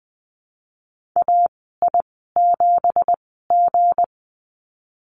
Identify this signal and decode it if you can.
Morse code, heard as a keyed tone, about 20 wpm, spaced as for 15 wpm, 700 Hz. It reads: AI7G
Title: 231 Lesson 37g - Letter x - Call Signs 20wpm